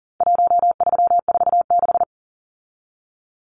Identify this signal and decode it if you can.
Morse code, heard as a keyed tone, about 40 wpm, 700 Hz. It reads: 1346
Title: Numbers - Contest Style - 4-Digit 40wpm